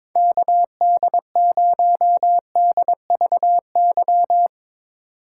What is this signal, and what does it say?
Morse code, heard as a keyed tone, about 22 wpm, 700 Hz. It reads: KD0DVY
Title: Speed Racing - USA 2x3 Call Signs 22wpm